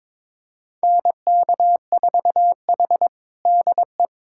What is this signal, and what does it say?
Morse code, heard as a keyed tone, about 22 wpm, 700 Hz. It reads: NK4H DE
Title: USA Callsign DE Callsign 22wpm